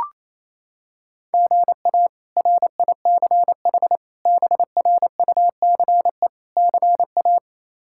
Morse code, heard as a keyed tone, about 27 wpm, 700 Hz. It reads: GA RICH BRUCE CA
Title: K1USN SST QSO Practice - 28wpm